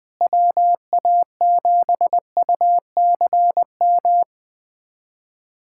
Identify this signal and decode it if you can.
Morse code, heard as a keyed tone, about 20 wpm, 700 Hz. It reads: WA7UCM
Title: USA Callsign DE Callsign - 6 Characters - 20wpm